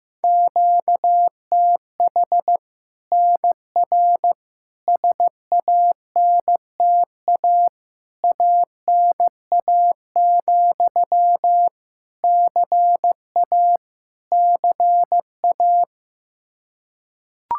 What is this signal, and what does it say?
Morse code, heard as a keyed tone, about 15 wpm, 700 Hz. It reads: QTH NR SANTA ANA, CA CA
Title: Practice QSO - Exchange 1 - Beginning - 15wpm